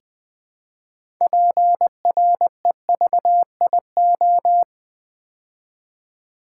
Morse code, heard as a keyed tone, about 20 wpm, 700 Hz. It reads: PREVIO
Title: Build A Word - 2 Letters At A Time - 8-Letter Words - 20wpm